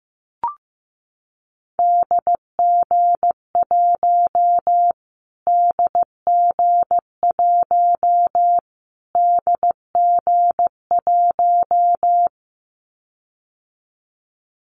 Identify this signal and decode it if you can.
Morse code, heard as a keyed tone, about 15 wpm, 700 Hz. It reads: DG1 DG1 DG1
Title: Callsign Prefixes for Top 25 DXCC - Repeated 3x - 15wpm